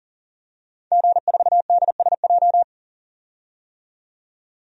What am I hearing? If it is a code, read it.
Morse code, heard as a keyed tone, about 38 wpm, 700 Hz. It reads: G4DSJ
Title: DXCC-100 Call Signs - 5 Characters - 40wpm